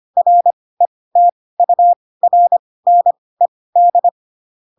Morse code, heard as a keyed tone, about 25 wpm, spaced as for 18 wpm, 700 Hz. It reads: RETURNED
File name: Pinocchio ch3 12 25 ik1hge